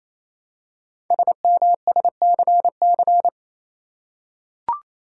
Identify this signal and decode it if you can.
Morse code, heard as a keyed tone, about 28 wpm, 700 Hz. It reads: SMSCC
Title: Code Groups - 5 Letters - 28wpm